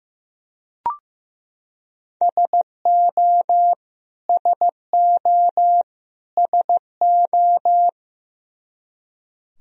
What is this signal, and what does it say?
Morse code, heard as a keyed tone, about 15 wpm, 700 Hz. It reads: SO SO SO
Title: Top 100 Words Repeated 15wpm